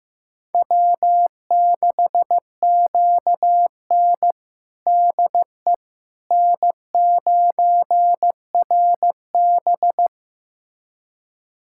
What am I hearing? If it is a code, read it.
Morse code, heard as a keyed tone, about 15 wpm, 700 Hz. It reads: W6QN DE N9RB
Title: USA Callsign DE Callsign - 4 Characters - 15wpm